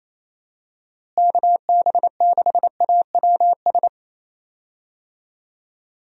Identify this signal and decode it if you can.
Morse code, heard as a keyed tone, about 28 wpm, 700 Hz. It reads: KB6AWS
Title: USA Call Signs - 6 Characters - Repeated 3x - 28wpm